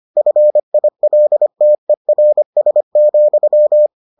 Morse code, heard as a keyed tone, about 25 wpm, 600 Hz. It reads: FILTERS,